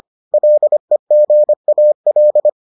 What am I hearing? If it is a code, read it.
Morse code, heard as a keyed tone, about 25 wpm, 600 Hz. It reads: LEGAL